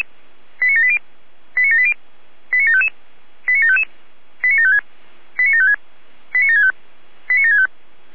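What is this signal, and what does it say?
Five-tone selective call (ZVEI), heard as quick runs of five tones, tones of about 70 ms. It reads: 89780 89780 89755 89755 89756 89756 89765 89765
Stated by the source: Five-tone ZVEI selective call received on an SDR.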